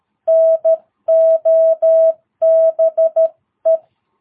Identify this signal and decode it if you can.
Morse code, heard as a keyed tone, about 13 wpm, 660 Hz. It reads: NOBE
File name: jh8cbh / 2022cw